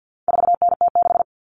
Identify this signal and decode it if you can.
Morse code, heard as a keyed tone, about 50 wpm, 700 Hz. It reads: 4NT6
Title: All Cut Numbers - 4-Digit 50wpm